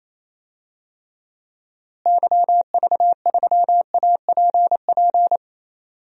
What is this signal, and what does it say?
Morse code, heard as a keyed tone, about 28 wpm, 700 Hz. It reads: YV3APP
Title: DXCC-100 Call Signs - 6 Characters - Repeated 2x - 28wpm